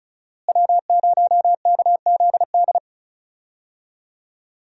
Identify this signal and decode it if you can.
Morse code, heard as a keyed tone, about 34 wpm, 700 Hz. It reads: W0KZD
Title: USA Call Signs - 5 Characters - Repeated 2x - 35wpm